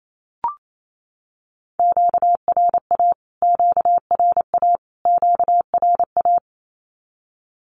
Morse code, heard as a keyed tone, about 28 wpm, 700 Hz. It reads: QRA QRA QRA
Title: Q-Codes Repeated 3x 28wpm